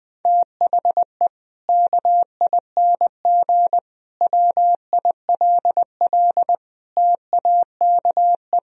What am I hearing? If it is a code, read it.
Morse code, heard as a keyed tone, about 20 wpm, 700 Hz. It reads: THE KING WILL TAKE
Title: Sentences from Top 1000 Words - Review Part 4 - 20wpm